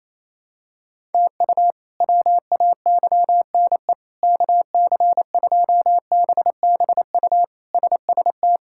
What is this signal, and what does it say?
Morse code, heard as a keyed tone, about 28 wpm, 700 Hz. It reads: TU WAYNE KC2BBU SST
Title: K1USN SST QSO Practice - 28wpm